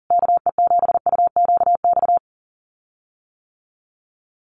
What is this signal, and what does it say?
Morse code, heard as a keyed tone, about 40 wpm, 700 Hz. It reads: KE7UQX
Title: USA Call Signs Repeated 2x 40wpm